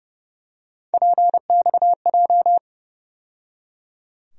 Morse code, heard as a keyed tone, about 30 wpm, 700 Hz. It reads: PXJ
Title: USA Call Sign Postfixes - 3 Letters 30wpm